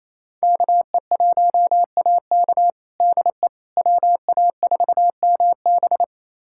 Morse code, heard as a keyed tone, about 27 wpm, 700 Hz. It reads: KE1AK DE WA4MB
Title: USA Callsign DE Callsign - 5 Characters - 28wpm